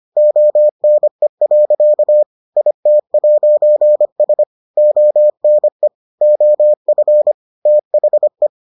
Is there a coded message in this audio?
Yes: Morse code from 0.2 to 8.5 s, heard as a keyed tone, about 25 wpm, 600 Hz. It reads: ONE. IT'S ONE OF THE